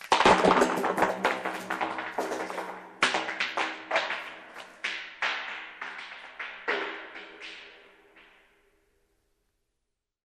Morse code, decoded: none